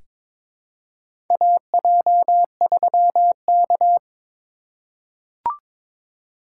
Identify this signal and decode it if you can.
Morse code, heard as a keyed tone, about 22 wpm, 700 Hz. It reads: AJ3K
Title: USA Call Signs - 4 Characters - 4x Character Spacing - 22wpm